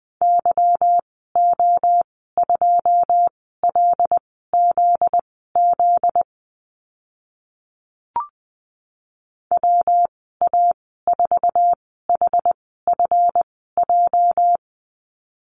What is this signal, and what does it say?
Morse code, heard as a keyed tone, about 20 wpm, spaced as for 14 wpm, 700 Hz. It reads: YO2LZZ WA4HFJ
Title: 245 Lesson 39g - Letter z - Call Signs 20wpm